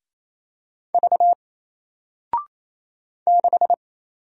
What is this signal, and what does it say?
Morse code, heard as a keyed tone, about 28 wpm, 700 Hz. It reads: V 6
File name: Single-Letter-Number-28wpm